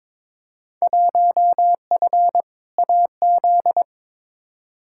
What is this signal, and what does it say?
Morse code, heard as a keyed tone, about 22 wpm, 700 Hz. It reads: 1F AZ
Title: Field Day Exchanges - 22wpm